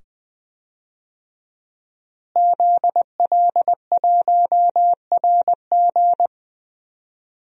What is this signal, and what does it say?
Morse code, heard as a keyed tone, about 20 wpm, 700 Hz. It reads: ZL1RG
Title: DXCC-100 Call Signs - 5 Characters - Repeated 2x - 20wpm